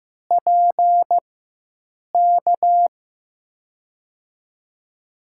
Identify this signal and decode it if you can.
Morse code, heard as a keyed tone, about 15 wpm, spaced as for 7 wpm, 700 Hz. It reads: PK
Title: USA Call Signs - 5 Characters - 4x Character Spacing - 15wpm